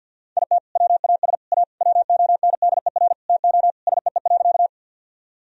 Morse code, heard as a keyed tone, about 50 wpm, 700 Hz. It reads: IT WAS A WONDER TO SEE.
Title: Sentences from Top 1000 Words - Review Part 3 - 50wpm